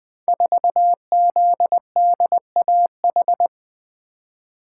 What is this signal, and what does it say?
Morse code, heard as a keyed tone, about 20 wpm, 700 Hz. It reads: 4ZDAH
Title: Code Groups - 5 Letter:Numbers - 20wpm